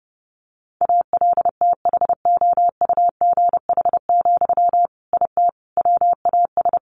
Morse code, heard as a keyed tone, about 30 wpm, 700 Hz. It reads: ALTHOUGH, IT WAS
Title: Sentences from Top 600 Words 30wpm